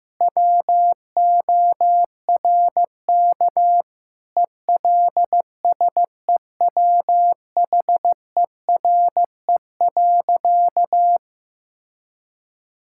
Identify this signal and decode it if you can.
Morse code, heard as a keyed tone, about 15 wpm, 700 Hz. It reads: WORK ELSEWHERE.
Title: Sentences from Top 1600 Words 15wpm